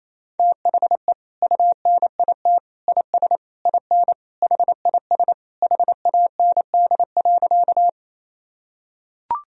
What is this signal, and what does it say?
Morse code, heard as a keyed tone, about 28 wpm, 700 Hz. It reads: THE UNIT IS IN HIS HAND.
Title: Sentences from Top 1000 Words - Review Part 2 - 28wpm